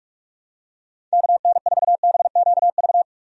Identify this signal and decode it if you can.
Morse code, heard as a keyed tone, about 46 wpm, 700 Hz. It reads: KN4BQV
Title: USA Call Signs - 6 Characters - 45wpm